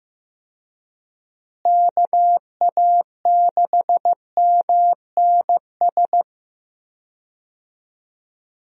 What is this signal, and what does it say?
Morse code, heard as a keyed tone, about 15 wpm, 700 Hz. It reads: KA6MNS
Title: USA Call Signs - 6 Characters - Repeated 3x - 15wpm